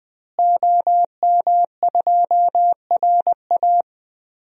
Morse code, heard as a keyed tone, about 20 wpm, 700 Hz. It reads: OM2RA
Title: DXCC-100 Call Signs - 5 Characters - Repeated 2x - 20wpm